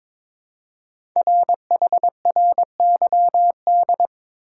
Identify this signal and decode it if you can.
Morse code, heard as a keyed tone, about 22 wpm, 700 Hz. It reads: RHRYD
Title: Code Groups - 5 Letters - 22wpm